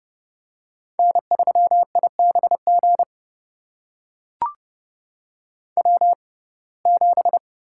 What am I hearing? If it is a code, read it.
Morse code, heard as a keyed tone, about 31 wpm, spaced as for 22 wpm, 700 Hz. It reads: N3IBG W7
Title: USA Call Signs - 5 Characters - 6x Character Spacing - 30wpm